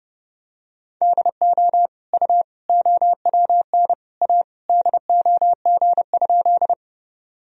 Morse code, heard as a keyed tone, about 32 wpm, 700 Hz. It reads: DO U OWN A DOG?
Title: Sentences from Top 400 Words and Abbreviations 30wpm